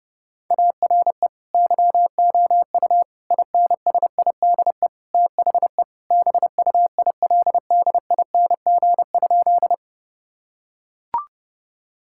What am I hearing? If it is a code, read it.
Morse code, heard as a keyed tone, about 30 wpm, 700 Hz. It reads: ARE YOU INSIDE THE BUILDING?
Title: Sentences from Top 1000 Words - Review Part 6 - 30wpm